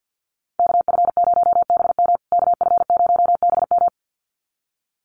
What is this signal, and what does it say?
Morse code, heard as a keyed tone, about 50 wpm, 700 Hz. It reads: KF0BM KF0BM
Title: USA Call Signs Repeated 2x 50wpm